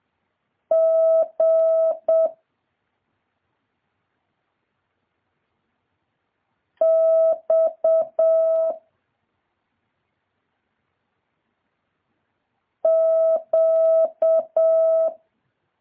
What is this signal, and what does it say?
Morse code, heard as a keyed tone, about 7 wpm, 650 Hz. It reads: G X Q
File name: jh8cbh / 2023cw